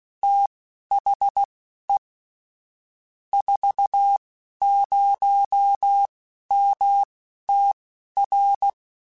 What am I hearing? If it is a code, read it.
Morse code, heard as a keyed tone, about 16 wpm, spaced as for 11 wpm, 800 Hz. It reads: THE 40MTR